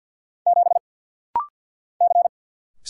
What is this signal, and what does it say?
Morse code, heard as a keyed tone, about 48 wpm, 700 Hz. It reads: 7 C
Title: Single Letter-Number - Rapid-Fire 50wpm